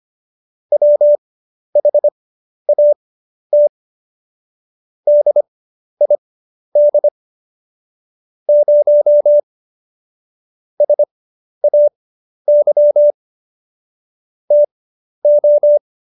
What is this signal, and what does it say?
Morse code, heard as a keyed tone, about 25 wpm, spaced as for 11 wpm, 590 Hz. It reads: WHAT DID 0 SAY TO
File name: Jokes-25-char-10-eff